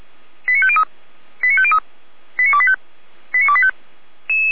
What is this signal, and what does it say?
Five-tone selective call (ZVEI), heard as quick runs of five tones, tones of about 70 ms. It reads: 89503 89503 89285 89285
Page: /455/SDR/ZVEI/89280/